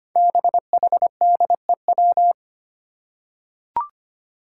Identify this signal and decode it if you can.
Morse code, heard as a keyed tone, about 25 wpm, 700 Hz. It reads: BHDEW
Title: Five Challenging Letter-Number Combinations 25wpm